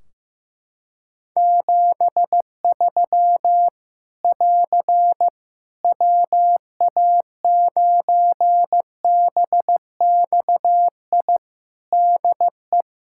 Morse code, heard as a keyed tone, about 15 wpm, 700 Hz. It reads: 73 <AR> WA9BXI DE
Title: K1USN SST QSO Practice 15wpm